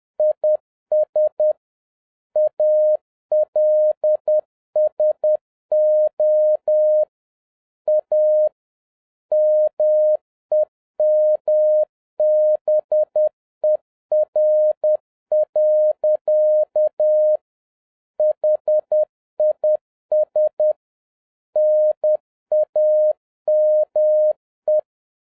Morse code, heard as a keyed tone, about 10 wpm, 610 Hz. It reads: IS ALSO A MEMBER. HIS NAME